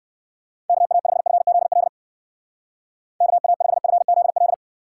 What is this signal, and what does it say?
Morse code, heard as a keyed tone, about 56 wpm, 700 Hz. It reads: KA5FZL KA5FZL
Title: Speed Racing - USA 2x3 Call Signs 45wpm